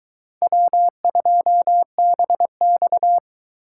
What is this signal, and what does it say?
Morse code, heard as a keyed tone, about 22 wpm, 700 Hz. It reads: W2BX